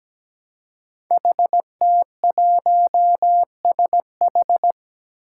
Morse code, heard as a keyed tone, about 17 wpm, 700 Hz. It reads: HT1SH
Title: Five Challenging Letter-Number Combinations 17wpm